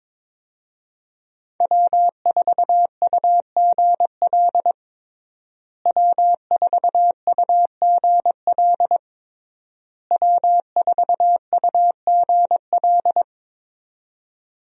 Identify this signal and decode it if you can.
Morse code, heard as a keyed tone, about 22 wpm, 700 Hz. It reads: W4UGL W4UGL W4UGL
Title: USA Call Signs - 5 Characters - Repeated 3x - 22wpm